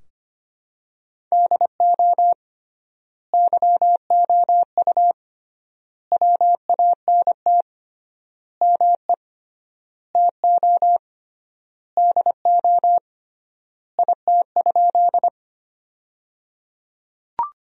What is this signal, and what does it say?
Morse code, heard as a keyed tone, about 25 wpm, 700 Hz. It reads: DO YOU WANT ME TO DO IT?
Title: 3x Word-Spacing - Sentences from Top 200 Words - 25wpm